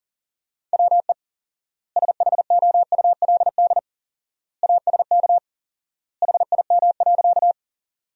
Morse code, heard as a keyed tone, about 40 wpm, 700 Hz. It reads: WE SHOULD ASK HIM.
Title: 4x Word-Spacing - Sentences from Top 300 Words - 40wpm